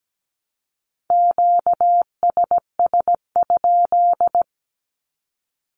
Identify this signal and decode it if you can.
Morse code, heard as a keyed tone, about 17 wpm, 700 Hz. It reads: QSS?
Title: Q-Codes Repeated 3x 17wpm